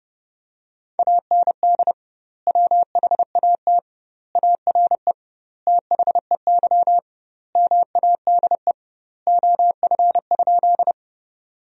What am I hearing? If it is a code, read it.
Morse code, heard as a keyed tone, about 30 wpm, 700 Hz. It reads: AND WHAT ARE THEY MADE OF?
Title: 2x Word-Spacing - Sentences from Top 200 Words - 30wpm